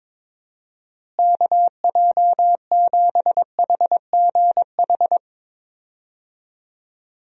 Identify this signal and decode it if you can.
Morse code, heard as a keyed tone, about 22 wpm, 700 Hz. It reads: KJ7HGH
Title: USA Call Signs - 6 Characters - Repeated 2x - 22wpm